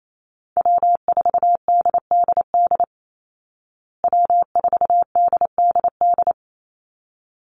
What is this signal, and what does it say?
Morse code, heard as a keyed tone, about 28 wpm, 700 Hz. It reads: W4DDD W4DDD
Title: Call Signs Repeated 3x - 28wpm